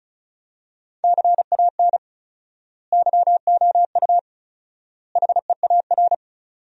Morse code, heard as a keyed tone, about 36 wpm, 700 Hz. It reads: CAN YOU HEAR
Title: 4x Word-Spacing - Sentences from Top 300 Words - 35wpm